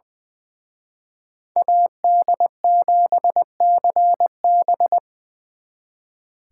Morse code, heard as a keyed tone, about 24 wpm, 700 Hz. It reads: AD7CB
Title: Speed Racing - USA 2x2 Call Signs 20wpm